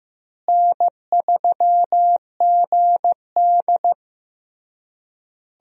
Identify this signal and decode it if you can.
Morse code, heard as a keyed tone, about 15 wpm, 700 Hz. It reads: N3GD
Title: USA Call Signs - 4 Characters - Repeated 3x - 15wpm